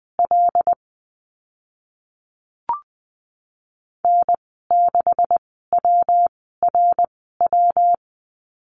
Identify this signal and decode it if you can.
Morse code, heard as a keyed tone, about 20 wpm, spaced as for 15 wpm, 700 Hz. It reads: L N6WRW
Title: 140 Lesson 23g - Letter f - Call Signs 20wpm